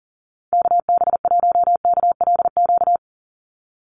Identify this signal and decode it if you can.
Morse code, heard as a keyed tone, about 40 wpm, 700 Hz. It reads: KB1KLQ
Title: USA Call Signs 40wpm